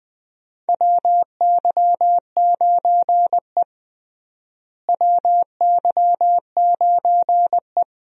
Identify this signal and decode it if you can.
Morse code, heard as a keyed tone, about 20 wpm, 700 Hz. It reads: WY9E WY9E
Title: USA Call Signs - 4 Characters - Repeated 2x - 20wpm